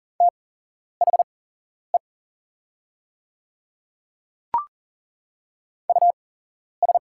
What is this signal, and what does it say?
Morse code, heard as a keyed tone, about 40 wpm, spaced as for 11 wpm, 700 Hz. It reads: THE US
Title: 8x Character Spacing - Top 200 Words - 40wpm